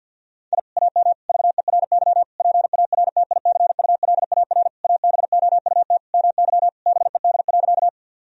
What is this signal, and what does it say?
Morse code, heard as a keyed tone, about 50 wpm, 700 Hz. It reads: I AM VERY PARTICULAR ABOUT MY BED.
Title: Sentences from Top 1000 Words - Review Part 4 - 50wpm